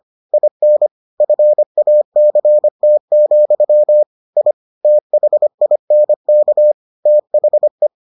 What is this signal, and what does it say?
Morse code, heard as a keyed tone, about 25 wpm, 600 Hz. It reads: IN FACT, I THINK THE